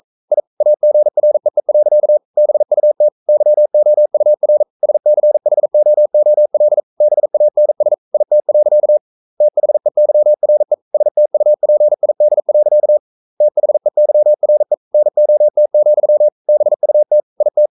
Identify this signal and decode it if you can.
Morse code, heard as a keyed tone, about 42 wpm, 600 Hz. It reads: I AGREE. BUT YOUR SCHOOL BANS IT. THEYRE STUPID. THEYRE NOT, BUT IT
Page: Romanen Never af Ken Follet på engelsk.